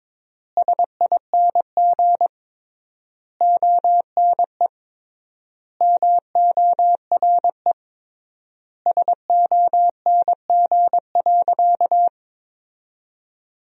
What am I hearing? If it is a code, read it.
Morse code, heard as a keyed tone, about 22 wpm, 700 Hz. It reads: SING ONE MORE SONG.
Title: 3x Word-Spacing - Sentences from Top 400 Words - 22wpm